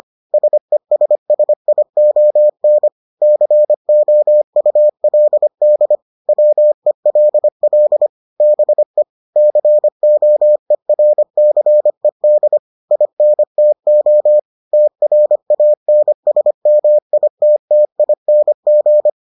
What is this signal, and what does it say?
Morse code, heard as a keyed tone, about 25 wpm, 600 Hz. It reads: SESSION COULD WELL BE COERCED INTO TRANSMITTING